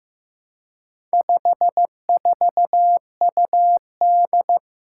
Morse code, heard as a keyed tone, about 15 wpm, 700 Hz. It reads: 54UD
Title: Four Challenging Letter-Number Combinations 15wpm